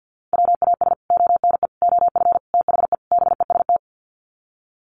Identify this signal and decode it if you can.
Morse code, heard as a keyed tone, about 50 wpm, 700 Hz. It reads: WAS ONE OF THE BEST
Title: Sets of 5 Words Encore2 50wpm